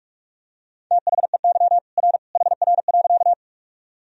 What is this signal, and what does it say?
Morse code, heard as a keyed tone, about 46 wpm, 700 Hz. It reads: THEY R HR.
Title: Sentences from Top 200 Words and Abbreviations 45wpm